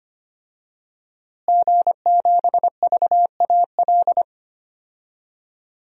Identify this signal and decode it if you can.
Morse code, heard as a keyed tone, about 25 wpm, 700 Hz. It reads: G7VAL
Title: DXCC-100 Call Signs - 5 Characters - Repeated 2x - 25wpm